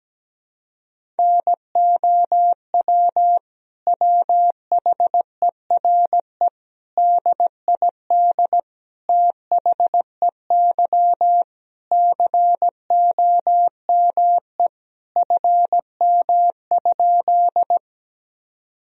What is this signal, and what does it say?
Morse code, heard as a keyed tone, about 17 wpm, 700 Hz. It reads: NOW WHERE DID THEY COME FM?
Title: Sentences from Top 200 Words and Abbreviations 17wpm